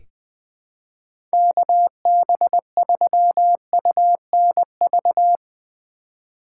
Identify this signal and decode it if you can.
Morse code, heard as a keyed tone, about 20 wpm, 700 Hz. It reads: KB3UNV
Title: USA Call Signs - 6 Characters - 2x Character Spacing - 20wpm